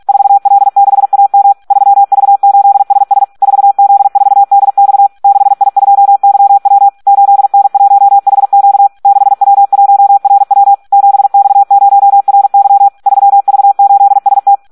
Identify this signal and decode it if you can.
Morse code, heard as a keyed tone, about 46 wpm, 800 Hz. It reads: 4L6AM 3V8SS 4Z4DX 6I2YW 8N1HQ 6W1RW 7X0RY 3V8ST
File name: Audio practica CW / SD 1 46